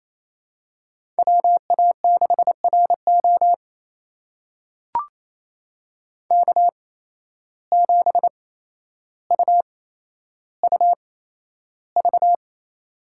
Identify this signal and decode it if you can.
Morse code, heard as a keyed tone, about 28 wpm, spaced as for 12 wpm, 700 Hz. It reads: WA6RO K7UUV
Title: USA Call Signs - 5 Characters - 8x Character Spacing - 28wpm